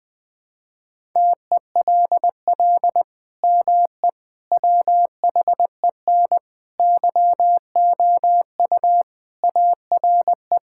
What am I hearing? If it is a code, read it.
Morse code, heard as a keyed tone, about 20 wpm, 700 Hz. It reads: TELL ME WHEN YOU ARE
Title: Sentences with 2-Word Phrases - Spoken Beginning - 20wpm